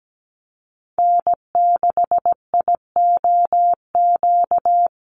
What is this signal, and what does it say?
Morse code, heard as a keyed tone, about 17 wpm, 700 Hz. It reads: N6IOQ
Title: USA Call Signs Repeated 3x 17wpm